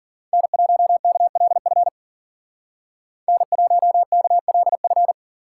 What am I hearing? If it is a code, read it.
Morse code, heard as a keyed tone, about 47 wpm, 700 Hz. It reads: N1KLF N1KLF
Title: Speed Racing - USA 1x3 Call Signs 40wpm